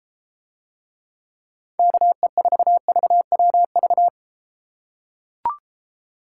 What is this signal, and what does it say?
Morse code, heard as a keyed tone, about 33 wpm, 700 Hz. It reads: KE4VWV